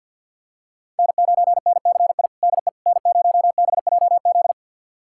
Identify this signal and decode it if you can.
Morse code, heard as a keyed tone, about 52 wpm, 690 Hz. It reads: N9NCI DE N0BJZ
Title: USA Callsign DE Callsign - 5 Characters - 50wpm